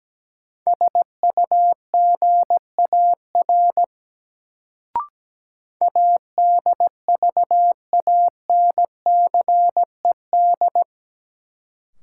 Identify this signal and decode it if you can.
Morse code, heard as a keyed tone, about 17 wpm, 700 Hz. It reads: SUGAR ADVANCED
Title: Sentences from Top 1200 Words - New Word Drill Alt - 17wpm